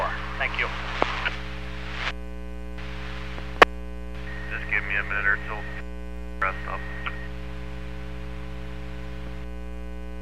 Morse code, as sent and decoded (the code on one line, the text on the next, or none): none